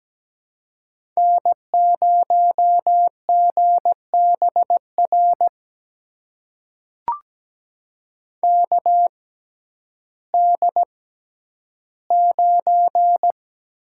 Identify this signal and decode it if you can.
Morse code, heard as a keyed tone, about 17 wpm, spaced as for 10 wpm, 700 Hz. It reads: N0GBR KD9
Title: USA Call Signs - 6x Character Spacing - 17wpm